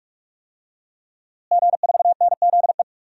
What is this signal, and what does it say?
Morse code, heard as a keyed tone, about 46 wpm, 700 Hz. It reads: G4NZE
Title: DXCC-100 Call Signs - 5 Characters - Repeated 2x - 45wpm